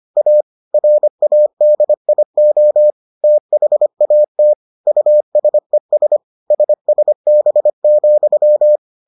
Morse code, heard as a keyed tone, about 25 wpm, 600 Hz. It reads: A RADIO THAT USES SSB,